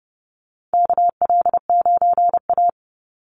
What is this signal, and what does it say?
Morse code, heard as a keyed tone, about 30 wpm, 700 Hz. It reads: KL9A
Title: Call Signs - 30wpm